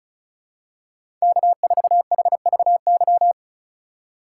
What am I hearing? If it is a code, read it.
Morse code, heard as a keyed tone, about 36 wpm, 700 Hz. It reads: K4HVY
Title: USA Call Signs - 5 Characters - 2x Character Spacing - 35wpm